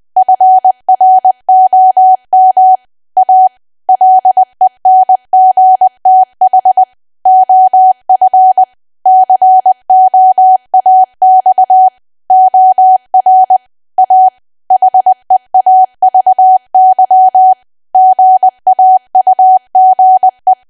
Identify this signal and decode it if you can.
Morse code, heard as a keyed tone, about 20 wpm, 750 Hz. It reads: FROM A LENGTH OF COAX OR A HEAVY GAUGE
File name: cwa / W1AW-Level3 / 20WPM